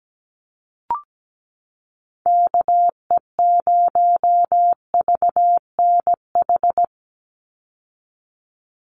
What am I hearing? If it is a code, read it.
Morse code, heard as a keyed tone, about 17 wpm, 700 Hz. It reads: KE0VNH
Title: USA Call Signs 17wpm